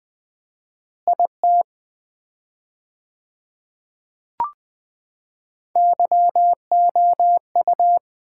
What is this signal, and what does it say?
Morse code, heard as a keyed tone, about 20 wpm, 700 Hz. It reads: IT YOU
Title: 4x Word-Spacing - Sets of 4 Words - 20wpm